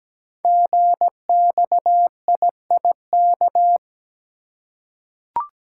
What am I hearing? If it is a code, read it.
Morse code, heard as a keyed tone, about 17 wpm, 700 Hz. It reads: GXIIK
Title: Code Groups - 5 Letters - 17wpm